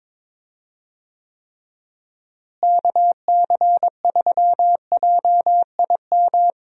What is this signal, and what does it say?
Morse code, heard as a keyed tone, about 21 wpm, 700 Hz. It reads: KC3JIM
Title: USA Call Signs - 6 Characters - Repeated 3x - 22wpm